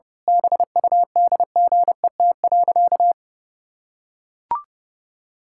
Morse code, heard as a keyed tone, about 30 wpm, 700 Hz. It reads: BUDGET.
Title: Pangram Challenge v2 - 30wpm